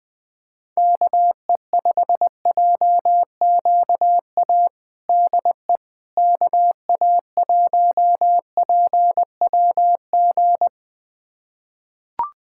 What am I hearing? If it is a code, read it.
Morse code, heard as a keyed tone, about 21 wpm, 700 Hz. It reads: KE5JQA DE KA1PWG
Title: USA Callsign DE Callsign 20wpm